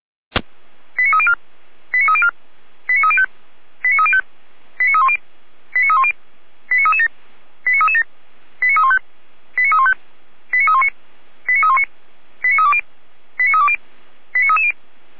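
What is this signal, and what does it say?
Five-tone selective call (ZVEI), heard as quick runs of five tones, tones of about 70 ms. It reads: 89304 89304 89305 89305 89310 89310 89337 89337 89315 89315 89319 89319 89320 89320 89330
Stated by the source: ZVEI five-tone call 89104 received by SDR, recorded by feuerwehr-orlamuende.org fire brigade server.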